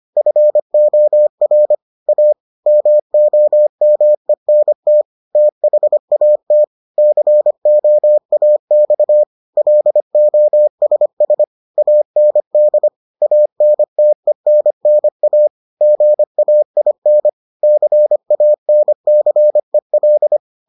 Morse code, heard as a keyed tone, about 25 wpm, 600 Hz. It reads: FOR A MOMENT THAT COAX LOSS AND ANTENNA GAIN CANCEL